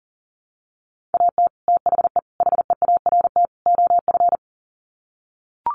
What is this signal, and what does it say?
Morse code, heard as a keyed tone, about 40 wpm, 700 Hz. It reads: AT THE HEART OF
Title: Sets of 4 Words 40wpm